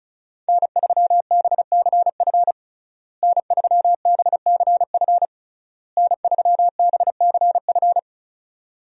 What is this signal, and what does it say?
Morse code, heard as a keyed tone, about 36 wpm, 700 Hz. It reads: N3BCF N3BCF N3BCF
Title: USA Call Signs - 5 Characters - Repeated 3x - 35wpm